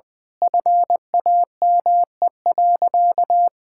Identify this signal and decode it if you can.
Morse code, heard as a keyed tone, about 20 wpm, 700 Hz. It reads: FAME.
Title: Sentences with 6-Word Phrases - Spoken Beginning - 20wpm